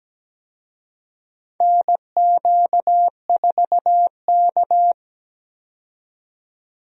Morse code, heard as a keyed tone, about 17 wpm, 700 Hz. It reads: NQ4K
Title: USA Call Signs - 4 Characters - 17wpm